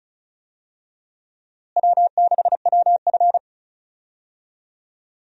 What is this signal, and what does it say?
Morse code, heard as a keyed tone, about 36 wpm, 700 Hz. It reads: W6WF
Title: USA Call Signs - 4 Characters - Repeated 2x - 35wpm